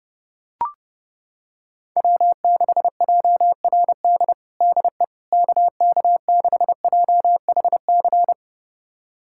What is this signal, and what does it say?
Morse code, heard as a keyed tone, about 30 wpm, 700 Hz. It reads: W6JRD DE KK6JHC
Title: USA Callsign DE Callsign 30wpm